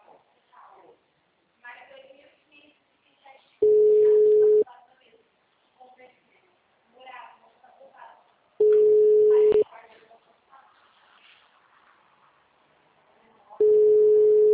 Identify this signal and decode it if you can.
Ringback tone (Europe, 425 Hz), with speech.